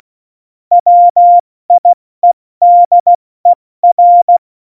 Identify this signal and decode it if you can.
Morse code, heard as a keyed tone, about 16 wpm, spaced as for 14 wpm, 700 Hz. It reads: WIEDER
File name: Morsekurs / Witze / Witz 27. Februar 2018, 70 F80